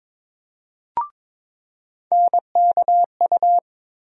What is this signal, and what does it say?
Morse code, heard as a keyed tone, about 21 wpm, 700 Hz. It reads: NKU
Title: USA Call Sign Postfixes - 3 Letters 22wpm